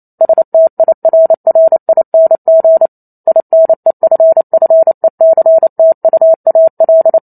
Morse code, heard as a keyed tone, about 28 wpm, 650 Hz. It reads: STIRRING INEFFECTUAL
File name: Chapter 2 - 28 WPM